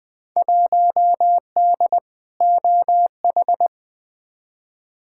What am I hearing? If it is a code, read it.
Morse code, heard as a keyed tone, about 20 wpm, 700 Hz. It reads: 1D OH
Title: Field Day Exchanges - 20wpm